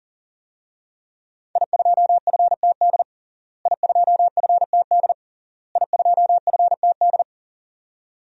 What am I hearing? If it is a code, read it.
Morse code, heard as a keyed tone, about 40 wpm, 700 Hz. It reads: I2FTD I2FTD I2FTD
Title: DXCC-100 Call Signs - 5 Characters - Repeated 3x - 40wpm